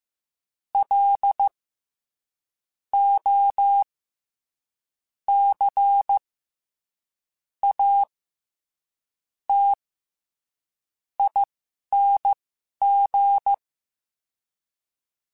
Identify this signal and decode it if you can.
Morse code, heard as a keyed tone, about 15 wpm, spaced as for 6 wpm, 800 Hz. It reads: LOCATING
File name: Special INGs-mp3s / Special INGs-mp3s / SlowToFaster ing @10